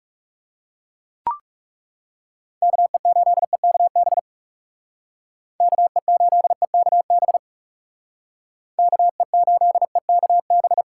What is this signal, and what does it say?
Morse code, heard as a keyed tone, about 39 wpm, 700 Hz. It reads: KE8EKB KE8EKB KE8EKB